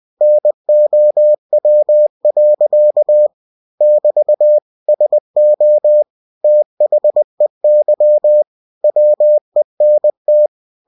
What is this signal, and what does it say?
Morse code, heard as a keyed tone, about 20 wpm, 600 Hz. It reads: NOW. <BT> SO THEY WENT
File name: cw-books / winnie-the-pooh / winnie-20wpm-page-0031